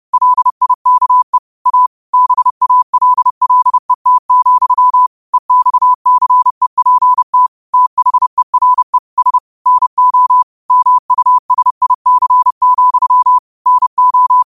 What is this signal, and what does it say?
Morse code, heard as a keyed tone, about 30 wpm, 1000 Hz. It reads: LIKE A BALLET, EXCEPT THERE S NO MUSIC, NO